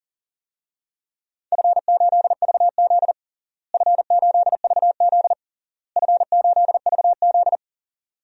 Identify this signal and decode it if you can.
Morse code, heard as a keyed tone, about 40 wpm, 700 Hz. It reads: F8VZ F8VZ F8VZ
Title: DXCC-100 Call Signs - 4 Characters - Repeated 3x - 40wpm